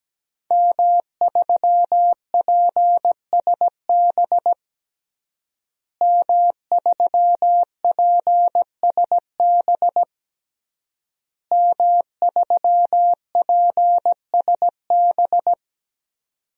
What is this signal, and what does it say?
Morse code, heard as a keyed tone, about 17 wpm, 700 Hz. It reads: M3PSB M3PSB M3PSB
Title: DXCC-100 Call Signs - 5 Characters - Repeated 3x - 17wpm